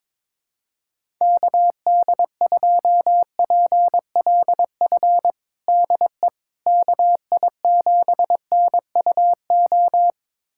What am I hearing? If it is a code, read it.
Morse code, heard as a keyed tone, about 22 wpm, 700 Hz. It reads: KD2PLF DE KI7NUO
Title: USA Callsign DE Callsign - 6 Characters - 22wpm